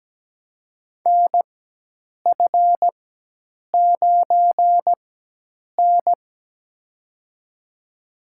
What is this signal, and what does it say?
Morse code, heard as a keyed tone, about 17 wpm, spaced as for 8 wpm, 700 Hz. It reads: NF9N
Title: USA Call Signs - 4 Characters - 4x Character Spacing - 17wpm